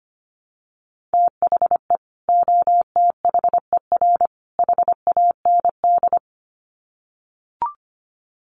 Morse code, heard as a keyed tone, about 25 wpm, 700 Hz. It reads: THE OTHER HAND
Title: Sets of 3 Words 25wpm